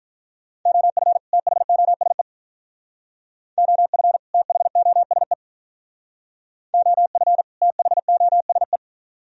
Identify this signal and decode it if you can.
Morse code, heard as a keyed tone, about 46 wpm, 700 Hz. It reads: OF THOSE OF THOSE OF THOSE